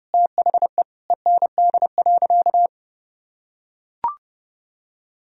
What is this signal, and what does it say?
Morse code, heard as a keyed tone, about 30 wpm, 700 Hz. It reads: THE END.
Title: Sentences from Top 1000 Words - Review Part 2 - 30wpm